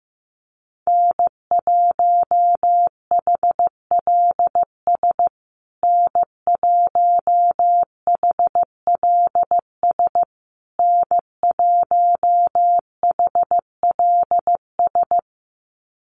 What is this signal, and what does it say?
Morse code, heard as a keyed tone, about 14 wpm, 700 Hz. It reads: N1HLS N1HLS N1HLS
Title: USA Call Signs Repeated 3x 15wpm